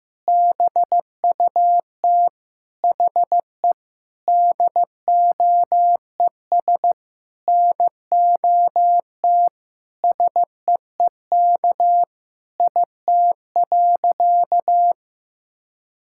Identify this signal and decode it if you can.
Morse code, heard as a keyed tone, about 15 wpm, 700 Hz. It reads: BUT HE DOES NOT SEEK IT.
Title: Sentences from Top 1600 Words 15wpm